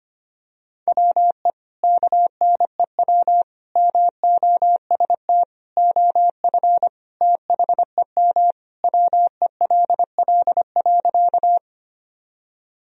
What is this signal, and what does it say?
Morse code, heard as a keyed tone, about 25 wpm, 700 Hz. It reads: WE KNEW MOST OF THEM WELL.
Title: Sentences from Top 1000 Words - Review Part 1 - 25wpm